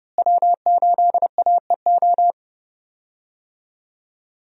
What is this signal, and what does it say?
Morse code, heard as a keyed tone, about 30 wpm, 700 Hz. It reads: W8AEO
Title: USA Call Signs - 5 Characters - Repeated 2x - 30wpm